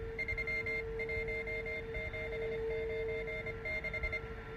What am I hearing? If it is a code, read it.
Morse code, heard as a keyed tone, about 25 wpm, 2100 Hz. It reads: 31796